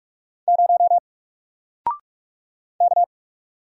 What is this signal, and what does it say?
Morse code, heard as a keyed tone, about 44 wpm, 700 Hz. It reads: 0 K
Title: Single Character (Letter:Number:Punctuation:Prosign) 45wpm